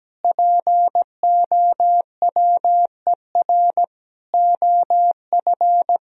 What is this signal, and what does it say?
Morse code, heard as a keyed tone, about 17 wpm, 700 Hz. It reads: POWER OF
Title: Sentences from Top 1000 Words - Review Part 3 - 17wpm